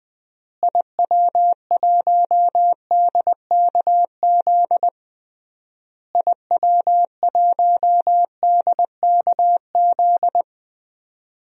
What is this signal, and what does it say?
Morse code, heard as a keyed tone, about 20 wpm, 700 Hz. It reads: IW1DKZ IW1DKZ
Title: DXCC-100 Call Signs - 6 Characters - Repeated 3x - 20wpm